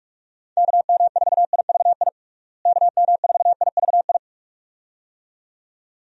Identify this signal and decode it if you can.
Morse code, heard as a keyed tone, about 41 wpm, 700 Hz. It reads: KM4IVI KM4IVI
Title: USA Call Signs - 6 Characters - Repeated 2x - 45wpm